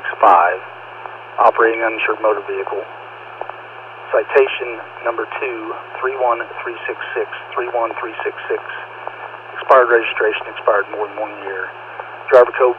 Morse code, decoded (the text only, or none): none